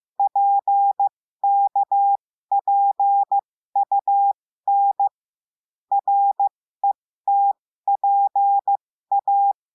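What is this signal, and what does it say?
Morse code, heard as a keyed tone, about 15 wpm, spaced as for 13 wpm, 800 Hz. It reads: PKPUN RETPA
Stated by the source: Morse lesson adding the letter T, with five-letter practice groups.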